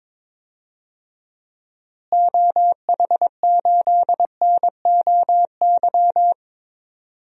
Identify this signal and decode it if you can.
Morse code, heard as a keyed tone, about 22 wpm, 700 Hz. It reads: OH8NOY
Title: DXCC-100 Call Signs - 6 Characters - Repeated 3x - 22wpm